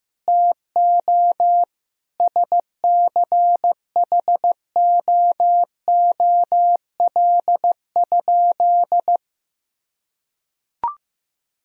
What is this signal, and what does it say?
Morse code, heard as a keyed tone, about 15 wpm, 700 Hz. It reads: TO SCHOOL?
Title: Sentences from Top 1000 Words - Review Part 1 - 15wpm